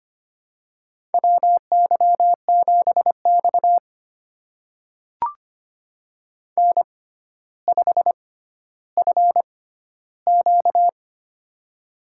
Morse code, heard as a keyed tone, about 25 wpm, spaced as for 13 wpm, 700 Hz. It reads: WY7X N5FQ
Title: USA Call Signs - 4 Characters - 6x Character Spacing - 25wpm